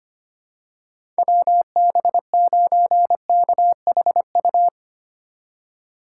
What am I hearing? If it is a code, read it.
Morse code, heard as a keyed tone, about 25 wpm, 700 Hz. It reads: WB9KHU
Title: USA Call Signs - 6 Characters - 3x Character Spacing - 25wpm